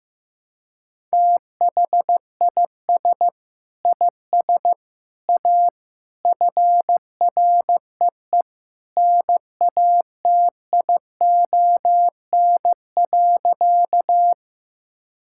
Morse code, heard as a keyed tone, about 15 wpm, 700 Hz. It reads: THIS IS A FREE NATION.
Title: Sentences from Top 1000 Words - Review Part 6 - 15wpm